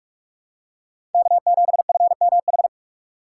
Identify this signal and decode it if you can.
Morse code, heard as a keyed tone, about 44 wpm, 700 Hz. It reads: K7FMH
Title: USA Call Signs - 5 Characters - 6x Character Spacing - 45wpm